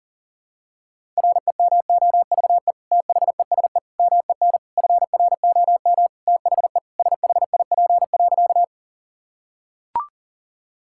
Morse code, heard as a keyed tone, about 40 wpm, 700 Hz. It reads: REMOVE THESE MEN FROM THE SHIP.
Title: Sentences from Top 1000 Words - Review Part 5 - 40wpm